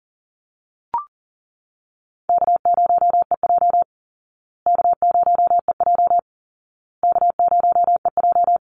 Morse code, heard as a keyed tone, about 40 wpm, 700 Hz. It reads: K0EJ K0EJ K0EJ
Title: Easy Call Signs Repeated 3x - 40wpm